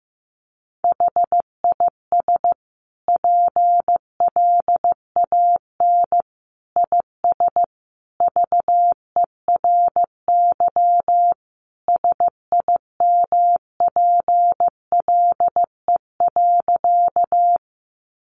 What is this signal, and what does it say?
Morse code, heard as a keyed tone, about 15 wpm, 700 Hz. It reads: HIS PLAN IS VERY SIMPLE.
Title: Sentences from Top 500 Words 15wpm